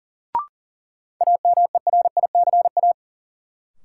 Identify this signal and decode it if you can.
Morse code, heard as a keyed tone, about 40 wpm, 700 Hz. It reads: AMERICA
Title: Sets of 5 Words - New Word Drill - 40wpm